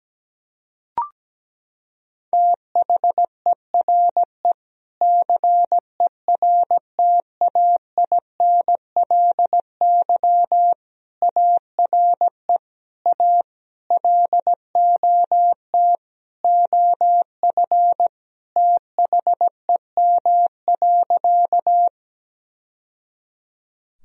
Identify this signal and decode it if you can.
Morse code, heard as a keyed tone, about 17 wpm, 700 Hz. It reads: THERE CERTAINLY ARE A LOT OF THEM.
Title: Sentences from Top 1000 Words - Review Part 3 - 17wpm